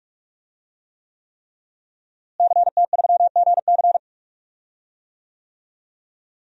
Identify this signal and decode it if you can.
Morse code, heard as a keyed tone, about 41 wpm, 700 Hz. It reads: CT3GC
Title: DXCC-100 Call Signs - 5 Characters - Repeated 3x - 45wpm